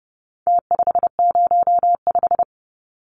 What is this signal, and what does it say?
Morse code, heard as a keyed tone, about 30 wpm, 700 Hz. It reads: T505
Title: Numbers - Contest Style - 4-Digit 30wpm